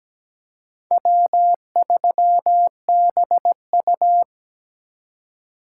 Morse code, heard as a keyed tone, about 17 wpm, 700 Hz. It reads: W3BU
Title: USA Call Signs - 4 Characters - 3x Character Spacing - 17wpm